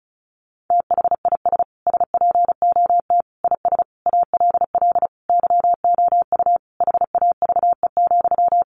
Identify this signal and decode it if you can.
Morse code, heard as a keyed tone, about 34 wpm, 700 Hz. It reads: THIS SPOT IS ALL YOU HAVE,
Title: Sentences from Top 1100 Words 35wpm